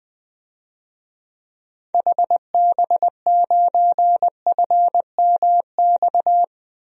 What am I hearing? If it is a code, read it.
Morse code, heard as a keyed tone, about 20 wpm, 700 Hz. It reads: HB9FMX
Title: DXCC-100 Call Signs - 6 Characters - Repeated 2x - 20wpm